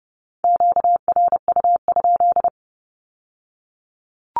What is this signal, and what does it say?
Morse code, heard as a keyed tone, about 30 wpm, 690 Hz. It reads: QRU?
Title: Q-Codes with Spoken Meaning - 30wpm